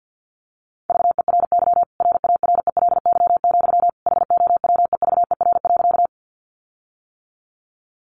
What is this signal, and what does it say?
Morse code, heard as a keyed tone, about 52 wpm, 700 Hz. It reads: VERY RARELY, HOWEVER.
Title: Sentences from Top 1400 Words 50wpm